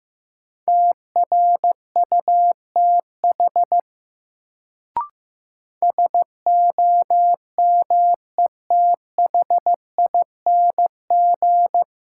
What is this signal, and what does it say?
Morse code, heard as a keyed tone, about 15 wpm, 700 Hz. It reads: TRUTH SOMETHING
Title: Sets of 5 Words - New Word Drill - 15wpm